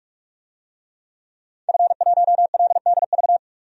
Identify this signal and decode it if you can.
Morse code, heard as a keyed tone, about 46 wpm, 700 Hz. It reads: F1LDV
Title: DXCC-100 Call Signs - 5 Characters - 45wpm